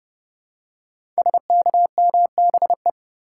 Morse code, heard as a keyed tone, about 30 wpm, 700 Hz. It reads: SKMBE
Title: Code Groups - 5 Letters - 30wpm